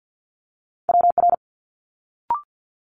Morse code, heard as a keyed tone, about 50 wpm, 700 Hz. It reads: WR
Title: Multi-letter Phonemes Quick Fire 50wpm